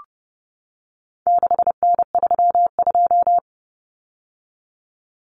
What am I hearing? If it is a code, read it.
Morse code, heard as a keyed tone, about 30 wpm, 700 Hz. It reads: 6N32
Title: Common Cut Numbers - 4-Digit 30wpm